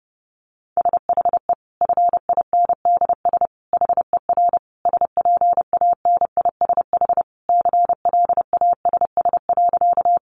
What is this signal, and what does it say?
Morse code, heard as a keyed tone, about 30 wpm, 700 Hz. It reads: SHE FINDS HER SPANISH CLASS.
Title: Additional Easy Phrases 30wpm